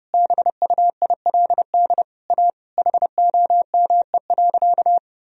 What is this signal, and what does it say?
Morse code, heard as a keyed tone, about 31 wpm, 700 Hz. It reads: BUILD A HOME.
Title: Sentences from Top 1000 Words - Review Part 1 - 30wpm